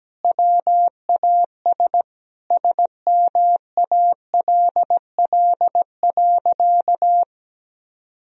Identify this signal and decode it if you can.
Morse code, heard as a keyed tone, about 17 wpm, 700 Hz. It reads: WAS SMALL.
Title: Sentences from Top 1000 Words - Review Part 1 - 17wpm